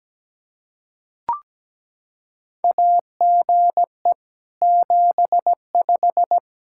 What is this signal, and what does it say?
Morse code, heard as a keyed tone, about 17 wpm, 700 Hz. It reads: AGE 75
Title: Practice QSO - Exchange 3 - More Info - 17wpm